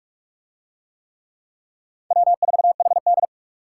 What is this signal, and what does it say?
Morse code, heard as a keyed tone, about 40 wpm, 700 Hz. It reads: W4HD